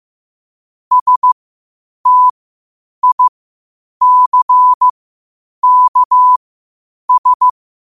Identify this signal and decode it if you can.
Morse code, heard as a keyed tone, about 15 wpm, spaced as for 8 wpm, 1000 Hz. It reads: STICKS
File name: qotdfiles / 8wpm